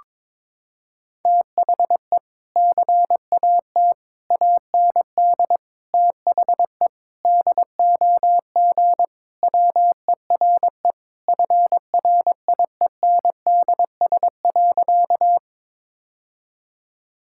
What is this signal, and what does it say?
Morse code, heard as a keyed tone, about 22 wpm, 700 Hz. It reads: THE CAT AND THE DOG WERE FRIENDS.
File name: practice / 22wpm / Sentences with 2-Word Phrases - 22wpm